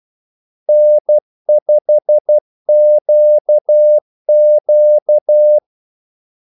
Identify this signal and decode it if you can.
Morse code, heard as a keyed tone, about 12 wpm, 600 Hz. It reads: N5QQ
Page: 2 ekstra ordmellemrum mellem kaldesignalerne